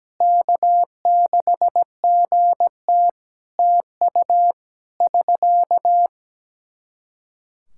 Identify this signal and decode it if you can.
Morse code, heard as a keyed tone, about 17 wpm, 700 Hz. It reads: K6GT TU <SK>
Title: Practice QSO - Exchange 4 - Ending - 17wpm